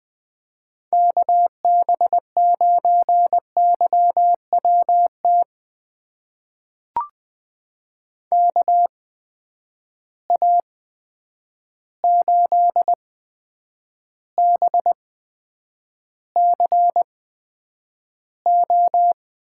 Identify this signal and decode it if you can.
Morse code, heard as a keyed tone, about 20 wpm, spaced as for 9 wpm, 700 Hz. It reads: KB9YWT KA8BCO
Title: USA Call Signs - 6 Characters - 8x Character Spacing - 20wpm